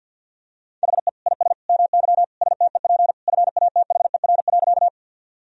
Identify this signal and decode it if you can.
Morse code, heard as a keyed tone, about 48 wpm, 700 Hz. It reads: HE IS MY STEP FATHER.
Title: Sentences from Top 1000 Words - Review Part 2 - 50wpm